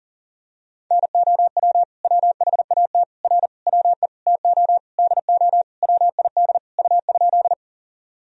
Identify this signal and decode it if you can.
Morse code, heard as a keyed tone, about 37 wpm, 700 Hz. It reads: NOW WHAT R WE TO DO WID U?
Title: Sentences from Top 100 Words and Abbreviations 40wpm